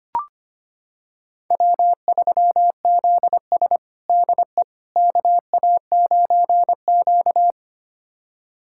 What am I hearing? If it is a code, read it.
Morse code, heard as a keyed tone, about 25 wpm, 700 Hz. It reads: W3ZS DE KA9Q
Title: USA Callsign DE Callsign - 4 Characters - 25wpm